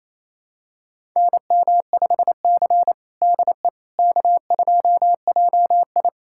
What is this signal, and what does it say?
Morse code, heard as a keyed tone, about 28 wpm, 700 Hz. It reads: NM5C DE K2JI
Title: USA Callsign DE Callsign - 4 Characters - 28wpm